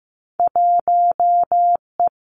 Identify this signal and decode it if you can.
Morse code, heard as a keyed tone, about 15 wpm, 700 Hz. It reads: 1 E
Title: Two Challenging Letter-Number Combinations 15wpm